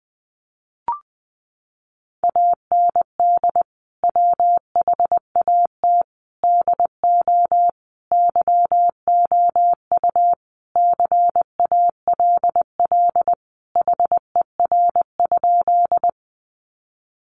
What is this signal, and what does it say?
Morse code, heard as a keyed tone, about 20 wpm, 700 Hz. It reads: AND WHAT DO YOU CALL HER?
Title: Sentences from Top 100 Words 20wpm